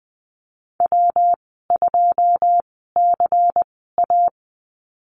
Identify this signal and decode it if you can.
Morse code, heard as a keyed tone, about 19 wpm, spaced as for 14 wpm, 700 Hz. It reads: W2CA
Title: 173 Lesson 28g - Number 9 - Call Signs 20wpm